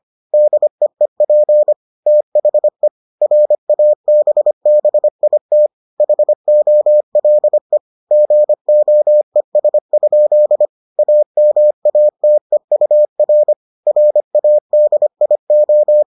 Morse code, heard as a keyed tone, about 25 wpm, 600 Hz. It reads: DEEP THE RABBIT HOLE GOES? AMATEUR RADIO